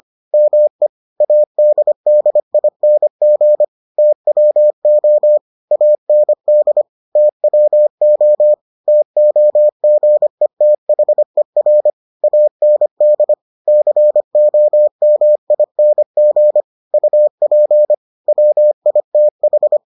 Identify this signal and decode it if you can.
Morse code, heard as a keyed tone, about 25 wpm, 600 Hz. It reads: ME ADDING TWO AND TWO TOGETHER AND COMING UP WITH